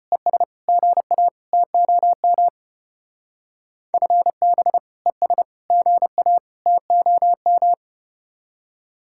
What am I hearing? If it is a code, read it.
Morse code, heard as a keyed tone, about 32 wpm, 700 Hz. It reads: ES GA TOM FB ES GA TOM